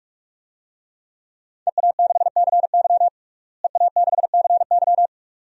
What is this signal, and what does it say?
Morse code, heard as a keyed tone, about 46 wpm, 700 Hz. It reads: EA6CY EA6CY
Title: DXCC-100 Call Signs - 5 Characters - Repeated 2x - 45wpm